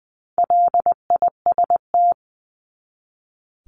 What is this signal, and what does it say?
Morse code, heard as a keyed tone, about 20 wpm, 700 Hz. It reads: LIST
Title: 260 Lesson 42e - Words 20-20wpm